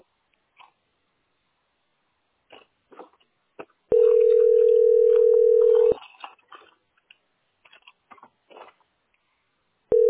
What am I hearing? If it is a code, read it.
Ringback tone (North America), 440 + 480 Hz.